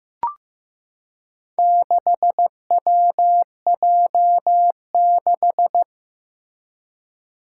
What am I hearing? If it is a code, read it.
Morse code, heard as a keyed tone, about 15 wpm, 700 Hz. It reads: 6WJ6
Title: Four Challenging Letter-Number Combinations 15wpm